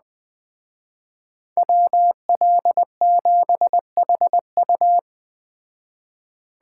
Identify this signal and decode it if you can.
Morse code, heard as a keyed tone, about 20 wpm, 700 Hz. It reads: WL7HU